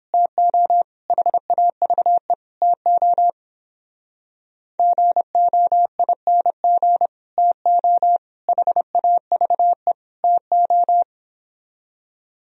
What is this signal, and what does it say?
Morse code, heard as a keyed tone, about 29 wpm, 700 Hz. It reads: TO HAVE TO GOING TO HAVE TO